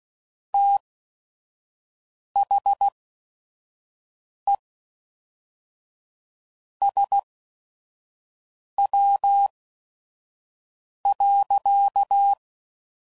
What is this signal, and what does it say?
Morse code, heard as a keyed tone, about 16 wpm, spaced as for 5 wpm, 800 Hz. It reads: THE SW.
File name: CW Code Groups @5WPM / Weather Sets / Weather 5 5